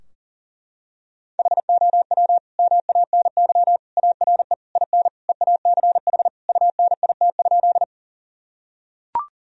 Morse code, heard as a keyed tone, about 40 wpm, 700 Hz. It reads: HOW MANY ARE IN EACH UNIT?
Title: Sentences from Top 1000 Words - Review Part 2 - 40wpm